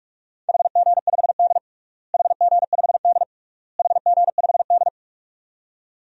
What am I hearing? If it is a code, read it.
Morse code, heard as a keyed tone, about 46 wpm, 700 Hz. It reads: HG5D HG5D HG5D
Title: DXCC-100 Call Signs - 4 Characters - Repeated 3x - 45wpm